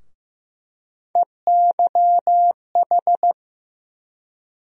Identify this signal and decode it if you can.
Morse code, heard as a keyed tone, about 15 wpm, 700 Hz. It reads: EYH
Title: USA Call Sign Postfixes - 3 Letters 15wpm